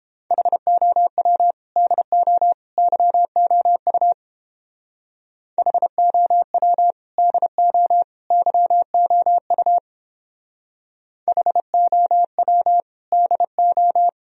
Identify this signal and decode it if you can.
Morse code, heard as a keyed tone, about 29 wpm, 700 Hz. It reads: HOW DO YOU HOW DO YOU HOW DO